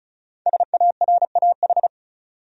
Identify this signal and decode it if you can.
Morse code, heard as a keyed tone, about 36 wpm, 700 Hz. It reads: SARAH
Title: USA Common Names - 35wpm